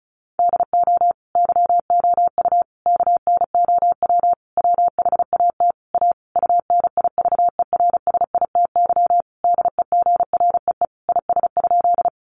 Morse code, heard as a keyed tone, about 34 wpm, 700 Hz. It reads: DO YOU KNOW WHAT A UNIVERSITY DEGREE IS?
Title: Sentences from Top 1500 Words 35wpm